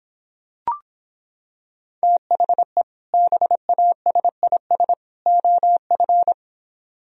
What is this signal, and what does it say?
Morse code, heard as a keyed tone, about 26 wpm, 700 Hz. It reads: THE BASIS OF